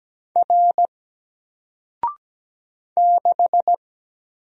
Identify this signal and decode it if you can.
Morse code, heard as a keyed tone, about 17 wpm, 700 Hz. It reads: R 6
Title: Single Letter-Number 17wpm